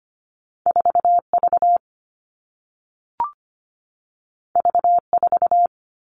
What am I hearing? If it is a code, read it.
Morse code, heard as a keyed tone, about 25 wpm, 700 Hz. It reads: 4V V4
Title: All Cut Numbers - 2-Digit 25wpm